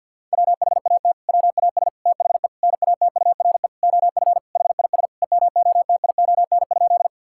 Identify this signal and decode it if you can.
Morse code, heard as a keyed tone, about 50 wpm, 700 Hz. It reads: WHAT WAS THE NATURE OF HIS EMOTION?
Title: Sentences from Top 1600 Words 50wpm